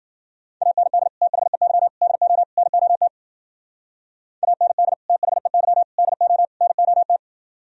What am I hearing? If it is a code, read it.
Morse code, heard as a keyed tone, about 52 wpm, 700 Hz. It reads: AND THEY DO NOT AND THEY DO NOT